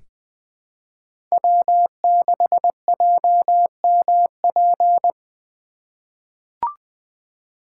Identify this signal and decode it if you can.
Morse code, heard as a keyed tone, about 21 wpm, 700 Hz. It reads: W6JMP
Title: 2023 SOTA Chaser Call Signs - 20wpm